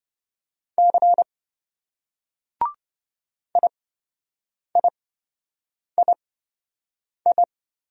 Morse code, heard as a keyed tone, about 27 wpm, spaced as for 7 wpm, 700 Hz. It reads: C IIII